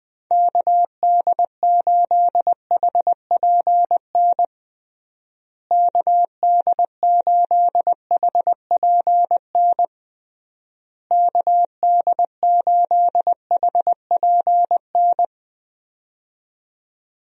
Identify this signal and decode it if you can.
Morse code, heard as a keyed tone, about 21 wpm, 700 Hz. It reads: KD8HPN KD8HPN KD8HPN
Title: USA Call Signs - 6 Characters - Repeated 3x - 20wpm